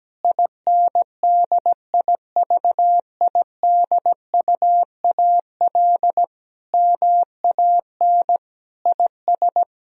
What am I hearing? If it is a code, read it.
Morse code, heard as a keyed tone, about 17 wpm, 700 Hz. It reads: INDIVIDUAL MAN IS
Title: Sentences from Top 1000 Words - Review Part 4 - 17wpm